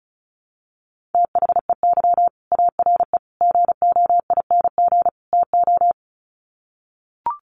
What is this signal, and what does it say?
Morse code, heard as a keyed tone, about 34 wpm, 700 Hz. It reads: THEY ARE GOING TO
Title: Sets of 4 Words 35wpm